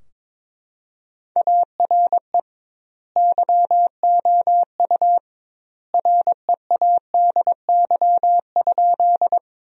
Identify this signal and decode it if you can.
Morse code, heard as a keyed tone, about 22 wpm, 700 Hz. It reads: ARE YOU READY?
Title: 2x Word-Spacing - Sentences from Top 400 Words - 22wpm